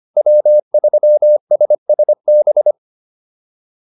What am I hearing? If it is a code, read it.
Morse code, heard as a keyed tone, about 25 wpm, 600 Hz. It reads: W3SSB